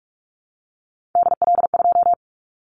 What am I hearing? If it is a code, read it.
Morse code, heard as a keyed tone, about 44 wpm, 700 Hz. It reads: DL2
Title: Callsign Prefixes for Top 25 DXCC - Repeated 3x - 45wpm